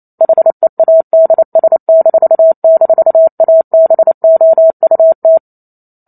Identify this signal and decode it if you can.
Morse code, heard as a keyed tone, about 28 wpm, 650 Hz. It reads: HEADS--ABOUT